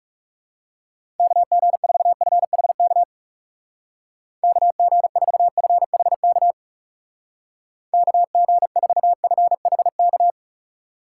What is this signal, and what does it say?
Morse code, heard as a keyed tone, about 39 wpm, 700 Hz. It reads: KG4FHK KG4FHK KG4FHK